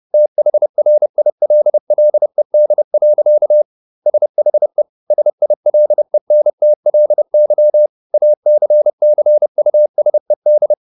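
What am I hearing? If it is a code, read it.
Morse code, heard as a keyed tone, about 30 wpm, 600 Hz. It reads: THRILLED. SHE SILENTLY ACCUSED
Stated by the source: Morse code at 30 wpm, the novel Never by Ken Follett in English, from CW med Gnister.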